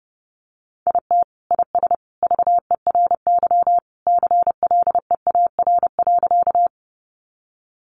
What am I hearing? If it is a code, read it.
Morse code, heard as a keyed tone, about 30 wpm, 700 Hz. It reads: IT IS VERY CLEAR.
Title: Sentences from Top 500 Words 30wpm